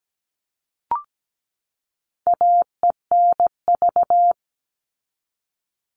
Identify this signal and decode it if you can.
Morse code, heard as a keyed tone, about 17 wpm, 700 Hz. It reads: AENV
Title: All Cut Numbers - 4-Digit 17wpm